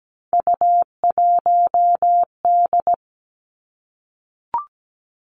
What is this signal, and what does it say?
Morse code, heard as a keyed tone, about 17 wpm, 700 Hz. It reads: U1D
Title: Three Challenging Letter-Number Combinations 17wpm